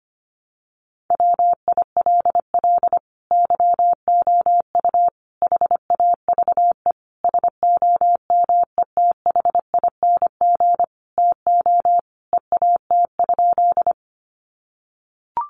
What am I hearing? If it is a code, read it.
Morse code, heard as a keyed tone, about 25 wpm, 700 Hz. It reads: WILL YOU HAVE SOMETHING TO EAT?
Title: Sentences from Top 600 Words 25wpm